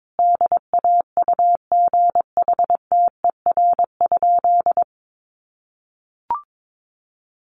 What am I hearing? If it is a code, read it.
Morse code, heard as a keyed tone, about 22 wpm, 700 Hz. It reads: DAUGHTER?
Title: Sentences from Top 1500 Words 22wpm